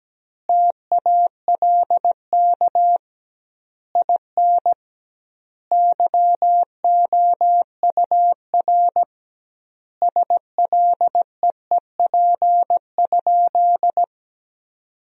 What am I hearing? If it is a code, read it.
Morse code, heard as a keyed tone, about 17 wpm, 700 Hz. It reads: TALK IN YOUR SLEEP?
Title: 2x Word-Spacing - Sentences from Top 500 Words - 17wpm